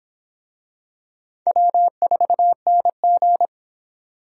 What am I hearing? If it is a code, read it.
Morse code, heard as a keyed tone, about 26 wpm, 700 Hz. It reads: W4NG